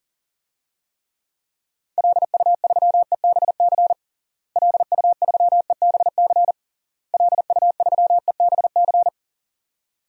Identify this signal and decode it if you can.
Morse code, heard as a keyed tone, about 40 wpm, 700 Hz. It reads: LU3EBC LU3EBC LU3EBC
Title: DXCC-100 Call Signs - 6 Characters - Repeated 3x - 40wpm